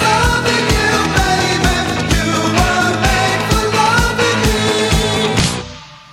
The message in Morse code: none